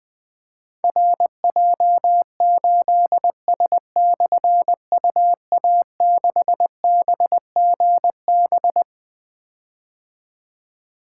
Morse code, heard as a keyed tone, about 20 wpm, 700 Hz. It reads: RJ8S/UA6BGB
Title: Difficult Call Signs 20wpm